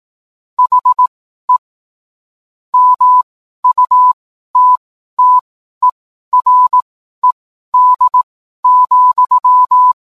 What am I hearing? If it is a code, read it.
Morse code, heard as a keyed tone, about 18 wpm, spaced as for 12 wpm, 1000 Hz. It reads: HE MUTTERED,